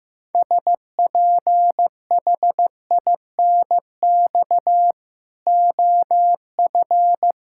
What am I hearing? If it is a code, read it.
Morse code, heard as a keyed tone, about 16 wpm, 700 Hz. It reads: SPHINX OF
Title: Pangram Challenge v2 - 15wpm